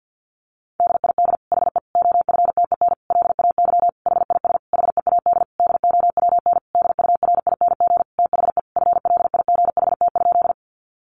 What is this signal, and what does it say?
Morse code, heard as a keyed tone, about 49 wpm, 700 Hz. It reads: DID HE OFTEN LAY HIS HEAD DOWN DURING THE FLIGHT?
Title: Sentences from Top 1000 Words 50wpm